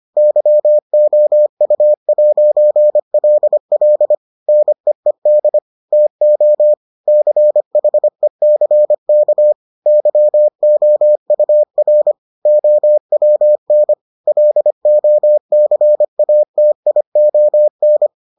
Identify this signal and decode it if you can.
Morse code, heard as a keyed tone, about 25 wpm, 600 Hz. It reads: YOU'LL NEED TO CHECK YOUR OWN LOCATION